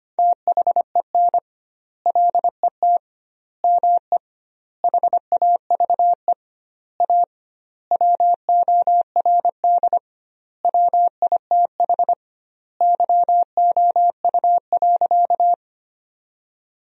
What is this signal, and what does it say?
Morse code, heard as a keyed tone, about 25 wpm, 700 Hz. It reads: THEN LET ME HAVE A WORD WITH YOU.
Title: 2x Word-Spacing - Sentences from Top 300 Words - 25wpm